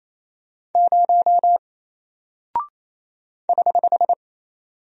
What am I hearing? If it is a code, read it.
Morse code, heard as a keyed tone, about 28 wpm, 700 Hz. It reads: 0 <HH>
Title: Single Character (Letter:Number:Punctuation:Prosign) Alt 28wpm